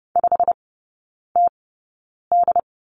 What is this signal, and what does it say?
Morse code, heard as a keyed tone, about 30 wpm, spaced as for 9 wpm, 700 Hz. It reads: 5TD
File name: course / 30:10wpm / 087 Lesson 15e - Number 5 - Three Characters 30wpm